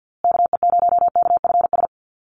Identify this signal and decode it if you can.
Morse code, heard as a keyed tone, about 50 wpm, 700 Hz. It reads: KE0KFS
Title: USA Call Signs Repeated 2x 50wpm